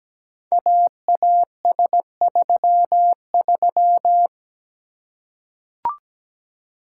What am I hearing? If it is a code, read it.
Morse code, heard as a keyed tone, about 17 wpm, 700 Hz. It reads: AAS33
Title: Code Groups - 5 Letter:Numbers - 17wpm